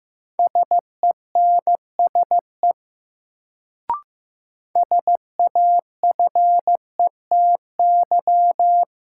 Morse code, heard as a keyed tone, about 15 wpm, 700 Hz. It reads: SENSE SAFETY
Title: Sentences from Top 700 Words - New Word Drill - 15wpm